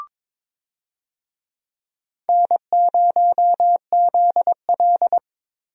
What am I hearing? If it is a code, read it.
Morse code, heard as a keyed tone, about 21 wpm, 700 Hz. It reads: N0ZL
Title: USA Call Signs - 4 Characters - Repeated 2x - 22wpm